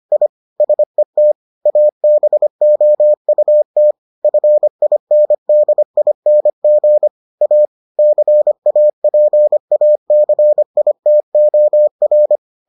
Morse code, heard as a keyed tone, about 25 wpm, 600 Hz. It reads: I SET ABOUT FINDING A CAPACITOR